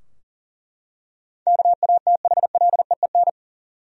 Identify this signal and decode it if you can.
Morse code, heard as a keyed tone, about 39 wpm, 700 Hz. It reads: KATHLEEN
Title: USA Common Names - 40wpm